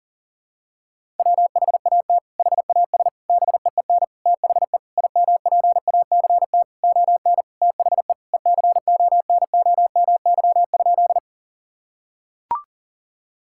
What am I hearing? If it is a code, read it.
Morse code, heard as a keyed tone, about 40 wpm, 700 Hz. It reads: WHAT HAS BEEN THE IMPACT ON THE ECONOMY?
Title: Sentences from Top 1000 Words - Review Part 6 - 40wpm